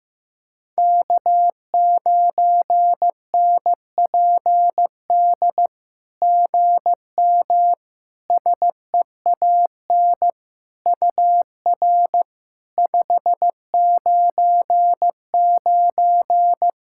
Morse code, heard as a keyed tone, about 15 wpm, 700 Hz. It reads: K9NPD GM SEAN UR 599
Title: SOTA QSO Practice 15wpm